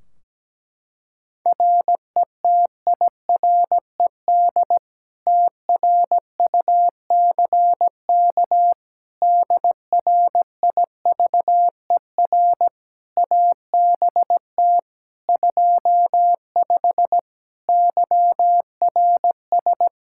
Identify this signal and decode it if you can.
Morse code, heard as a keyed tone, about 17 wpm, 700 Hz. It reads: RETIRED TRUCK DRIVER ABT 25 YRS
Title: Practice QSO - Exchange 3 - More Info - 17wpm